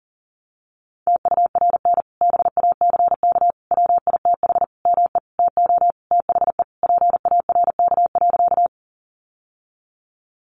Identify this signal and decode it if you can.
Morse code, heard as a keyed tone, about 40 wpm, 700 Hz. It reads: TURN BACK WITH ME TO THE PARK.
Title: Sentences from Top 1300 Words 40wpm